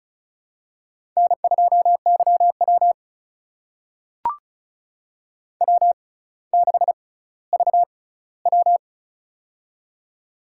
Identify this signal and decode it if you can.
Morse code, heard as a keyed tone, about 35 wpm, spaced as for 18 wpm, 700 Hz. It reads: N2YW W6VW
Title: USA Call Signs - 4 Characters - 6x Character Spacing - 35wpm